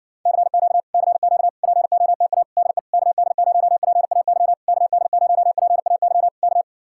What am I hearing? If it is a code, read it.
Morse code, heard as a keyed tone, about 66 wpm, 700 Hz. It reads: CQ CQ POTA DE KD0PAY KD0PAY K